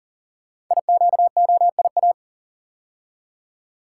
Morse code, heard as a keyed tone, about 43 wpm, 700 Hz. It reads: IQOIA
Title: Code Groups - 5 Letters - 40wpm